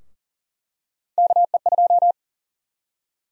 Morse code, heard as a keyed tone, about 40 wpm, 700 Hz. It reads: KE2
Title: USA Call Sign Prefixes - 40wpm